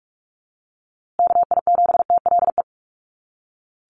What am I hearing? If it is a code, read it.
Morse code, heard as a keyed tone, about 44 wpm, 700 Hz. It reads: KI7TLE
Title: USA Call Signs Repeated 2x 45wpm